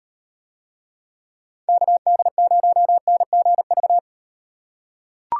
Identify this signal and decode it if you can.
Morse code, heard as a keyed tone, about 38 wpm, 700 Hz. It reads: KD0NGV